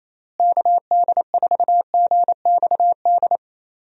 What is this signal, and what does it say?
Morse code, heard as a keyed tone, about 28 wpm, 700 Hz. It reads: KD4GXD
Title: USA Call Signs - 6 Characters - Repeated 2x - 28wpm